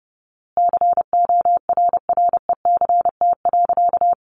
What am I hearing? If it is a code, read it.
Morse code, heard as a keyed tone, about 30 wpm, 700 Hz. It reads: CORRECT.
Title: A Sound Of Thunder 30wpm